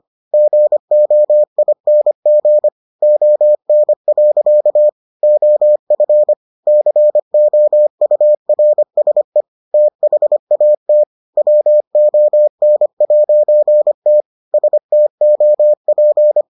Morse code, heard as a keyed tone, about 25 wpm, 600 Hz. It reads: GOING ON. OF COURSE THAT WON'T STOP